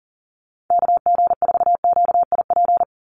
Morse code, heard as a keyed tone, about 40 wpm, 700 Hz. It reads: KG4QIP
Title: USA Call Signs 40wpm